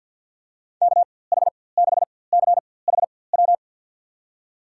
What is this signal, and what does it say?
Morse code, heard as a keyed tone, about 50 wpm, spaced as for 23 wpm, 700 Hz. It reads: KH6CHW
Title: USA Call Signs - 6 Characters - 4x Character Spacing - 50wpm